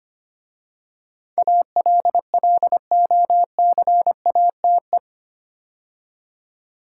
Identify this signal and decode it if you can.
Morse code, heard as a keyed tone, about 25 wpm, 700 Hz. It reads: ALLOCATE
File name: 8 Letter Words - First 3 Letters Given - 25wpm